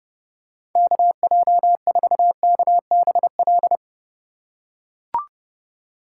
Morse code, heard as a keyed tone, about 32 wpm, 700 Hz. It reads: KJ4KBL